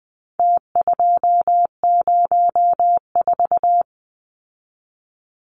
Morse code, heard as a keyed tone, about 20 wpm, 700 Hz. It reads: T204
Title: Numbers - Contest Style - 4-Digit 20wpm